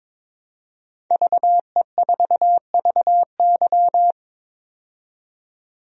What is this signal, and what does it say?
Morse code, heard as a keyed tone, about 22 wpm, 700 Hz. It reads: VE4VY
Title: DXCC-100 Call Signs - 5 Characters - 22wpm